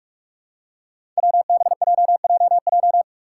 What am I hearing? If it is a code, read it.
Morse code, heard as a keyed tone, about 44 wpm, 700 Hz. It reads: WBJJJ
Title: Five Challenging Letter-Number Combinations 45wpm